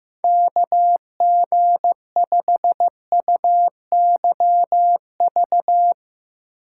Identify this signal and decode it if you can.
Morse code, heard as a keyed tone, about 15 wpm, 700 Hz. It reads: KG5UYV
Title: Speed Racing - USA 2x3 Call Signs 15wpm